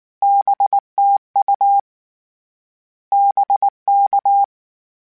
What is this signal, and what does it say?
Morse code, heard as a keyed tone, about 19 wpm, 800 Hz. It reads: BTU BK
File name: Short Runs / Short Runs-mp3s / Words @18